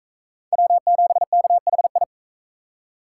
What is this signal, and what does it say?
Morse code, heard as a keyed tone, about 42 wpm, 700 Hz. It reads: W7KHI